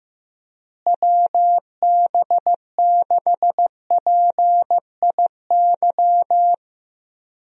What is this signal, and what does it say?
Morse code, heard as a keyed tone, about 15 wpm, 700 Hz. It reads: WB6PIY
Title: USA Call Signs - 6 Characters - Repeated 2x - 15wpm